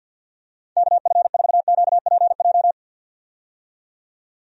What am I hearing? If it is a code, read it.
Morse code, heard as a keyed tone, about 52 wpm, 700 Hz. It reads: KF4QPJ
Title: USA Call Signs - 6 Characters - 50wpm